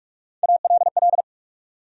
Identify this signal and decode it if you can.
Morse code, heard as a keyed tone, about 59 wpm, 700 Hz. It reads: ALL